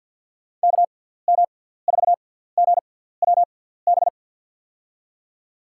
Morse code, heard as a keyed tone, about 46 wpm, spaced as for 17 wpm, 700 Hz. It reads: KM4GWB
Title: USA Call Signs - 6 Characters - 6x Character Spacing - 50wpm